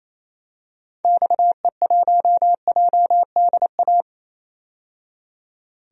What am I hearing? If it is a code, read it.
Morse code, heard as a keyed tone, about 28 wpm, 700 Hz. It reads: XE1JDA
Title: DXCC-100 Call Signs - 6 Characters - 28wpm